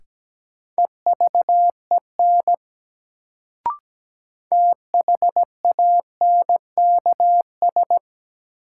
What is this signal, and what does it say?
Morse code, heard as a keyed tone, about 17 wpm, 700 Hz. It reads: EVEN THANKS
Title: Sentences from Top 600 Words - New Word Drill - 17wpm